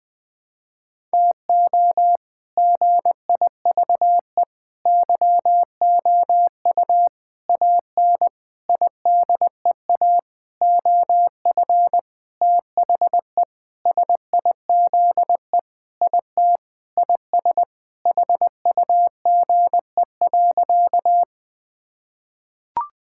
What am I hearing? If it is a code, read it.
Morse code, heard as a keyed tone, about 20 wpm, 700 Hz. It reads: TO GIVE YOU AN IDEA OF THE SIZE IT IS HUGE.
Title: Sentences with 6-Word Phrases - Spoken Beginning - 20wpm